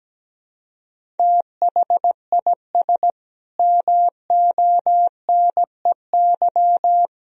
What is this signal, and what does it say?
Morse code, heard as a keyed tone, about 17 wpm, 700 Hz. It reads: THIS MONEY
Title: Sentences from Top 400 Words and Abbreviations 17wpm